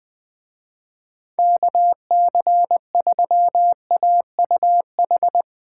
Morse code, heard as a keyed tone, about 21 wpm, 700 Hz. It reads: KC3AUH
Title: Speed Racing - USA 2x3 Call Signs 20wpm